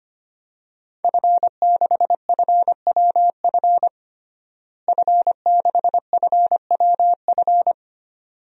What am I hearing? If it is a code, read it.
Morse code, heard as a keyed tone, about 25 wpm, 700 Hz. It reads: F6FWF F6FWF
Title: DXCC-100 Call Signs - 5 Characters - Repeated 2x - 25wpm